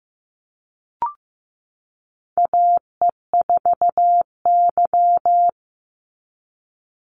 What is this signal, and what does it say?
Morse code, heard as a keyed tone, about 16 wpm, 700 Hz. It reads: AE4Y
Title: NA SOTA Chaser Callsigns 15wpm